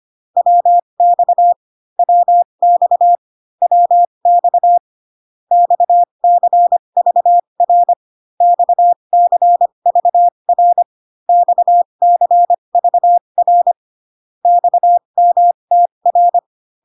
Morse code, heard as a keyed tone, about 25 wpm, spaced as for 22 wpm, 700 Hz. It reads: WX WX WX XCVR XCVR XCVR XMTR
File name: CW common abbreviations alphabetical repeated 3 times 25 18wpm